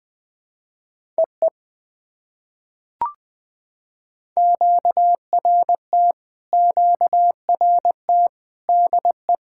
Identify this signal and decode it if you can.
Morse code, heard as a keyed tone, about 20 wpm, 700 Hz. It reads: EE QRT QRT DE
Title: POTA QSO Practice - 20wpm